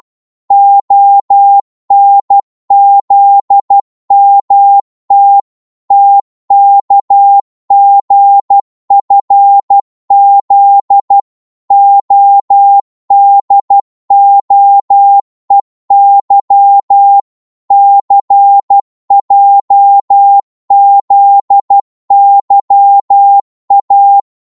Morse code, heard as a keyed tone, about 12 wpm, 800 Hz. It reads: ONZMT TKGFZ ODOEY CJZYA
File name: cwmp3 / ltr-12wpm